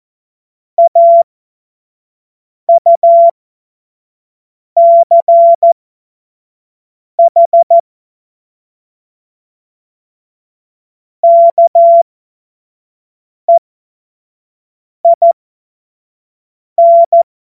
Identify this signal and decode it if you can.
Morse code, heard as a keyed tone, about 14 wpm, spaced as for 5 wpm, 680 Hz. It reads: AUCH KEIN
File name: Morsekurs / Witze / Witz 01. Juni 2021, 25 F70